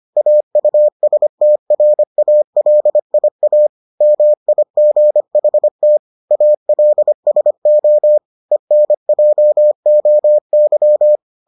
Morse code, heard as a keyed tone, about 25 wpm, 600 Hz. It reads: AUSTRALIA MIGHT ALSO ENJOY